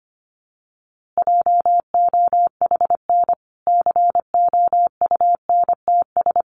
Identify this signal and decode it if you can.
Morse code, heard as a keyed tone, about 25 wpm, 700 Hz. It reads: JOHN COUNTS
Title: Easy Phrases 25wpm